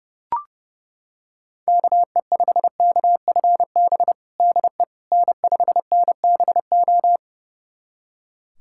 Morse code, heard as a keyed tone, about 32 wpm, 700 Hz. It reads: KE5KFB DE N5NBO
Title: USA Callsign DE Callsign 30wpm